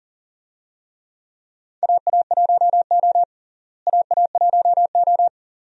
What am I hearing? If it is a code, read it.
Morse code, heard as a keyed tone, about 40 wpm, 700 Hz. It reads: AA1O AA1O
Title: USA Call Signs - 4 Characters - Repeated 2x - 40wpm